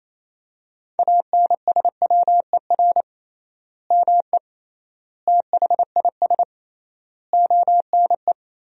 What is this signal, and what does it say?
Morse code, heard as a keyed tone, about 28 wpm, 700 Hz. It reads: ANSWER ME THIS ONE
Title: 3x Word-Spacing - Sentences from Top 400 Words - 28wpm